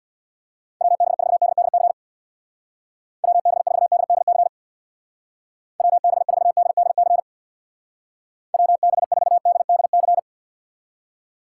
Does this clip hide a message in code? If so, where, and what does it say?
Yes: Morse code from 0.8 to 10.2 s, heard as a keyed tone, about 63 wpm, 700 Hz. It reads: WB4DDC WB4DDC WB4DDC WB4DDC